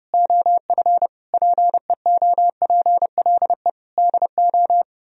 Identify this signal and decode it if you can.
Morse code, heard as a keyed tone, about 36 wpm, 700 Hz. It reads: OF PEOPLE DO
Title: Speed-Racing - Sets of 5 Words 30wpm